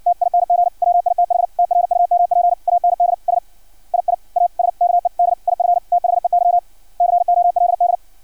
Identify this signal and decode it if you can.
Morse code, heard as a keyed tone, about 56 wpm, 700 Hz, with radio noise. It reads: TING WITH TRUMP AND S II ASKED IF THEY COLD
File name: archived / CRSnet